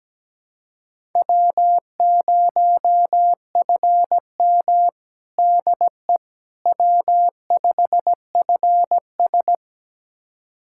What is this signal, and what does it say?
Morse code, heard as a keyed tone, about 17 wpm, 700 Hz. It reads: W0FM DE W5FS
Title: USA Callsign DE Callsign - 4 Characters - 17wpm